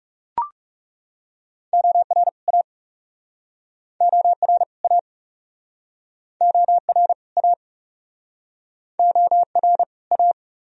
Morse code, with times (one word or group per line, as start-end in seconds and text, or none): 1.7-2.3 OR
2.5-2.6 A
4.0-4.6 OR
4.8-5.0 A
6.4-7.1 OR
7.4-7.5 A
9.0-9.8 OR
10.1-10.3 A